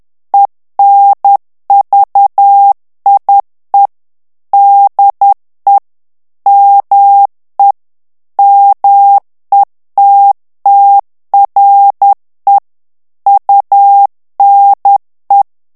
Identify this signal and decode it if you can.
Morse code, heard as a keyed tone, about 10 wpm, 800 Hz. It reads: ENVIE DE ME METTRE UNE
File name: FRANCAIS TEXTE CLAIR 2 VITESSE 10 MOTS MINUTES